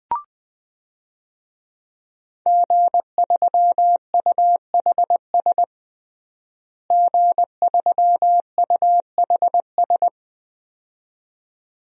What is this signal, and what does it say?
Morse code, heard as a keyed tone, about 21 wpm, 700 Hz. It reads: G3UHS G3UHS
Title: DXCC-100 Call Signs - 5 Characters - Repeated 2x - 20wpm